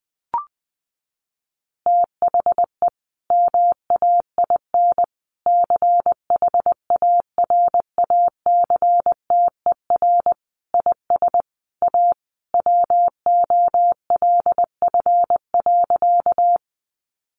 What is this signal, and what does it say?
Morse code, heard as a keyed tone, about 20 wpm, 700 Hz. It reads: THE MAIN CHARACTER IS A WOLF.
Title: Additional Easy Phrases 20wpm